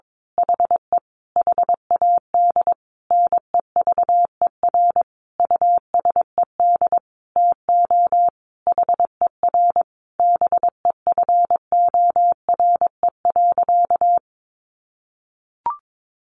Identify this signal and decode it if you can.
Morse code, heard as a keyed tone, about 22 wpm, 700 Hz. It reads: HE HAD NEVER USED TO HER BEFORE.
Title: Sentences from Top 600 Words 22wpm